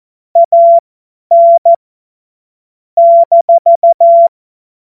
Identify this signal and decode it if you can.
Morse code, heard as a keyed tone, about 14 wpm, spaced as for 10 wpm, 680 Hz. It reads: AN -
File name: Morsekurs / Witze / Witz 19. April 2022 50 F70